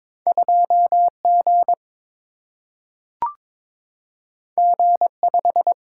Morse code, heard as a keyed tone, about 22 wpm, 700 Hz. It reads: 2G G5
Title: Single-Letter Single-Number 22wpm